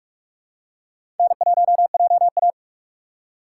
Morse code, heard as a keyed tone, about 46 wpm, 700 Hz. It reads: N1JA
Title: USA Call Signs - 4 Characters - 6x Character Spacing - 45wpm